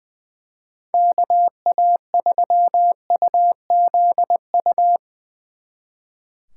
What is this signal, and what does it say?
Morse code, heard as a keyed tone, about 22 wpm, 700 Hz. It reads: KA3UZU
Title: Speed Racing - USA 2x3 Call Signs 20wpm